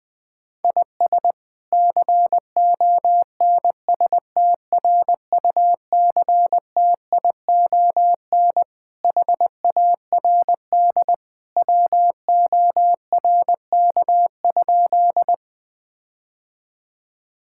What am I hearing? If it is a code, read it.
Morse code, heard as a keyed tone, about 20 wpm, 700 Hz. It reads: IS CONSTRUCTION HARD WORK?
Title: Sentences from Top 1600 Words 20wpm